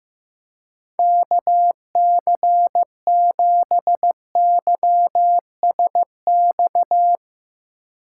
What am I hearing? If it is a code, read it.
Morse code, heard as a keyed tone, about 15 wpm, 700 Hz. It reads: KC7YSX
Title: USA Call Signs - 2x Character Spacing - 15wpm